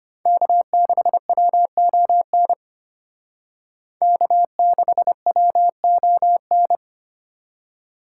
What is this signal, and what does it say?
Morse code, heard as a keyed tone, about 32 wpm, 700 Hz. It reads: K6WON K6WON